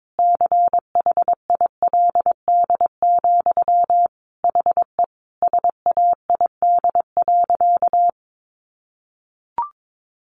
Morse code, heard as a keyed tone, about 22 wpm, 700 Hz. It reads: CHILD, HE SAID.
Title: Sentences from Top 800 Words 22wpm